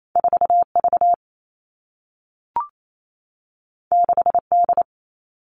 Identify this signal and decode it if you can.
Morse code, heard as a keyed tone, about 28 wpm, 700 Hz. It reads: 4 V 6 D
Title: Two Challenging Letter-Number Combinations 28wpm